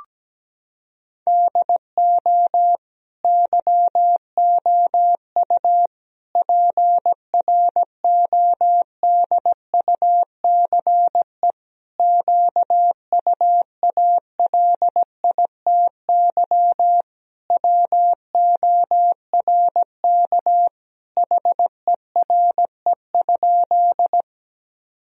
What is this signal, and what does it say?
Morse code, heard as a keyed tone, about 17 wpm, 700 Hz. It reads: DO YOU PRODUCE QUALITY WORK HERE?
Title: Sentences from Top 1000 Words - Review Part 4 - 17wpm